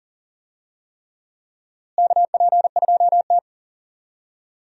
Morse code, heard as a keyed tone, about 40 wpm, 700 Hz. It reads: KP2T
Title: USA Call Signs - 4 Characters - Repeated 3x - 40wpm